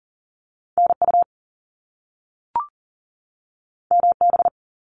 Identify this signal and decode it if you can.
Morse code, heard as a keyed tone, about 40 wpm, 700 Hz. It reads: NU MB
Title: Canadian Provinces and Territories 40wpm